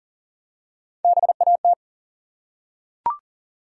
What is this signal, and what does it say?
Morse code, heard as a keyed tone, about 40 wpm, 700 Hz. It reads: BAT
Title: Three Letter Words - 40wpm